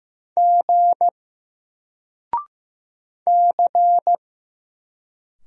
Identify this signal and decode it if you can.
Morse code, heard as a keyed tone, about 15 wpm, 700 Hz. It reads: G C
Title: Single Letter-Number 15wpm